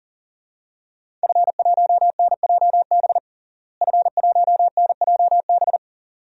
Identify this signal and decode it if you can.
Morse code, heard as a keyed tone, about 40 wpm, 700 Hz. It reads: F1NJB F1NJB
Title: DXCC-100 Call Signs - 5 Characters - Repeated 2x - 40wpm